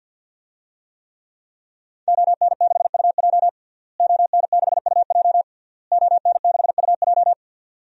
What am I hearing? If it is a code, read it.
Morse code, heard as a keyed tone, about 48 wpm, 700 Hz. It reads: ON6UJ ON6UJ ON6UJ
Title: DXCC-100 Call Signs - 5 Characters - Repeated 3x - 50wpm